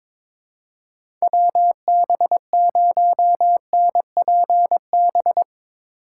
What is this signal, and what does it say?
Morse code, heard as a keyed tone, about 21 wpm, 700 Hz. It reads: WB0NPB
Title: USA Call Signs - 4x Character Spacing - 22wpm